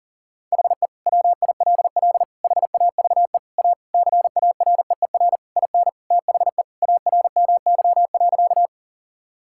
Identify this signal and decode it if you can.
Morse code, heard as a keyed tone, about 40 wpm, 700 Hz. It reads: HE WILL HAVE A CAREER IN THE ARMY.
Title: Sentences from Top 1000 Words - Review Part 6 - 40wpm